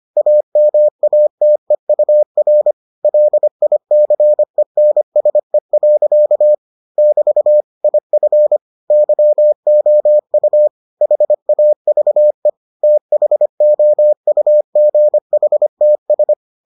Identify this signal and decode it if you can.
Morse code, heard as a keyed tone, about 25 wpm, 600 Hz. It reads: AMATEUR LICENSE. <BT> IF YOU HAVE THOUGHTS